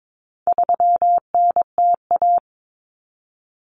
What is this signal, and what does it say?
Morse code, heard as a keyed tone, about 22 wpm, 700 Hz. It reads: 3NTA
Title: Common Cut Numbers - 4-Digit 22wpm